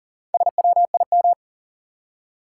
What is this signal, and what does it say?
Morse code, heard as a keyed tone, about 40 wpm, 700 Hz. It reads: SWIM
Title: Four Letter Words - 40wpm